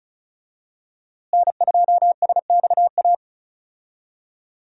Morse code, heard as a keyed tone, about 34 wpm, 700 Hz. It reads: N2SXA
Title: USA Call Signs - 5 Characters - Repeated 2x - 35wpm